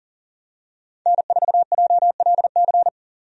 Morse code, heard as a keyed tone, about 40 wpm, 700 Hz. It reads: N4JLC
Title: USA Call Signs - 5 Characters - 40wpm